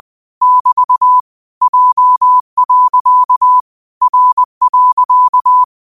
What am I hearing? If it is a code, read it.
Morse code, heard as a keyed tone, about 20 wpm, 1000 Hz. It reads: <BT> J. R.
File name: qotdfiles / 20wpm